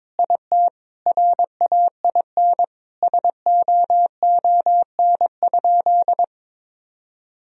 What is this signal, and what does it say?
Morse code, heard as a keyed tone, about 22 wpm, 700 Hz. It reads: IT RAIN SOON?
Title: Sentences from Top 1000 Words - Review Part 1 - 22wpm